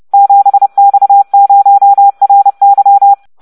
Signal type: Morse code